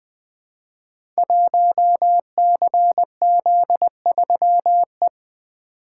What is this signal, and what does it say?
Morse code, heard as a keyed tone, about 20 wpm, 700 Hz. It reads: 1CZ3E
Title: Code Groups - 5 Letter:Numbers - 20wpm